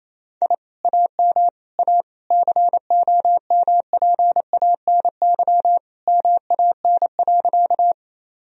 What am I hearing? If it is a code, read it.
Morse code, heard as a keyed tone, about 28 wpm, 700 Hz. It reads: I AM A COMPANY MAN.
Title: Sentences from Top 1000 Words - Review Part 3 - 28wpm